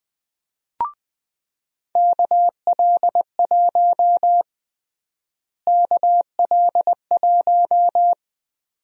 Morse code, heard as a keyed tone, about 20 wpm, 700 Hz. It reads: KL1 KL1
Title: USA Call Sign Prefixes - Repeated 2x - 20wpm